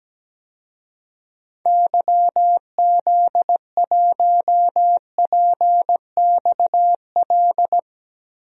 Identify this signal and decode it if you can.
Morse code, heard as a keyed tone, about 17 wpm, 700 Hz. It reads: YZ1PXL
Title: DXCC-100 Call Signs - 6 Characters - Repeated 3x - 17wpm